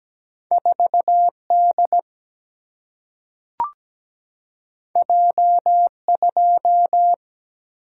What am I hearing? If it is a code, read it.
Morse code, heard as a keyed tone, about 17 wpm, 700 Hz. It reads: 4 D J 2
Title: Single-Letter Single-Number 17wpm